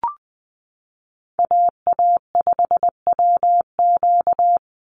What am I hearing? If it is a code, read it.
Morse code, heard as a keyed tone, about 19 wpm, 700 Hz. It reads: AA5WQ
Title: USA Call Signs 20wpm